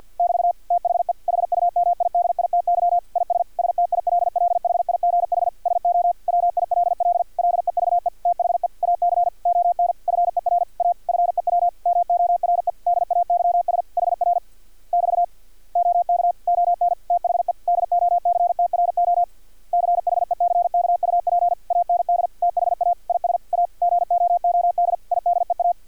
Morse code, heard as a keyed tone, about 50 wpm, 700 Hz. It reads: <BT> THE HUMIDITY IS STILLHIGH SO WILL LEVE THE AC ON FER A FEW MORE DAYS HR <BT> OK ON THE DOCTRO CHECKUP AND THA IS A GOOD IDEA
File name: archived / CRSnet